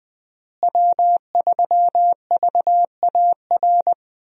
Morse code, heard as a keyed tone, about 21 wpm, 700 Hz. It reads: W3VAR
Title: USA Call Signs - 5 Characters - Repeated 3x - 20wpm